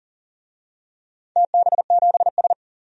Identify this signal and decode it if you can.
Morse code, heard as a keyed tone, about 38 wpm, 690 Hz. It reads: TB7S
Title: DXCC-100 Call Signs - 4 Characters - 40wpm